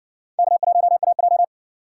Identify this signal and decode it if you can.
Morse code, heard as a keyed tone, about 60 wpm, 700 Hz. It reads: K1AJ